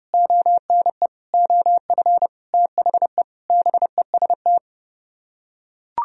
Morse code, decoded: ONE OF THE BEST

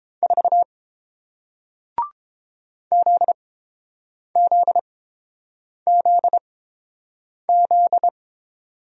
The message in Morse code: 4 ZZZZ

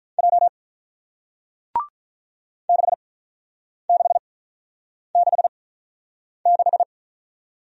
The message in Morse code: J 6666

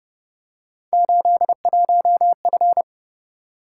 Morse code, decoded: 81F

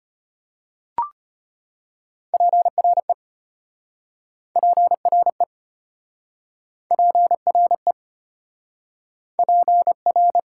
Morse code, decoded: PRE PRE PRE PR

